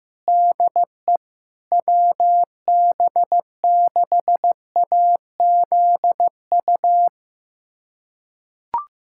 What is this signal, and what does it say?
Morse code, heard as a keyed tone, about 15 wpm, 700 Hz. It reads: DE WB6AZU
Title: USA Callsign DE Callsign - 6 Characters - 15wpm